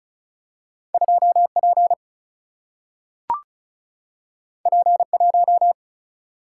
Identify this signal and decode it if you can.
Morse code, heard as a keyed tone, about 36 wpm, 700 Hz. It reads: 2P P1
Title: Single-Letter Single-Number 35wpm